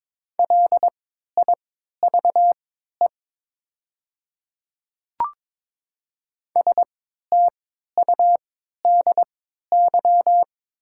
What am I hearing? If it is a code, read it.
Morse code, heard as a keyed tone, about 22 wpm, spaced as for 12 wpm, 700 Hz. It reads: LIVE STUDY
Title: 3x Character Spacing - Top 500 Words - 22wpm